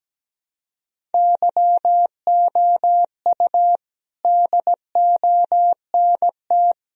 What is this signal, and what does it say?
Morse code, heard as a keyed tone, about 17 wpm, 700 Hz. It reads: YOU DONT
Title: Sentences from Top 1000 Words - Review Part 1 - 17wpm